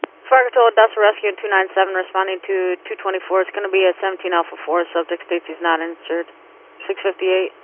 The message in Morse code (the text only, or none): none